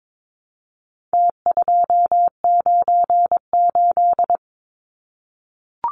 Morse code, heard as a keyed tone, about 22 wpm, 700 Hz. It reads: T298
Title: Numbers - Contest Style - 4-Digit 22wpm